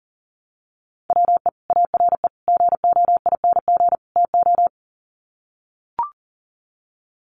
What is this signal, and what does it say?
Morse code, heard as a keyed tone, about 40 wpm, 700 Hz. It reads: WE ARE GOING TO
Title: Sets of 4 Words 40wpm